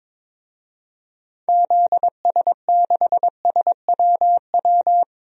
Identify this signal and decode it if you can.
Morse code, heard as a keyed tone, about 22 wpm, 700 Hz. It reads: ZS6SWW
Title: DXCC-100 Call Signs - 6 Characters - 22wpm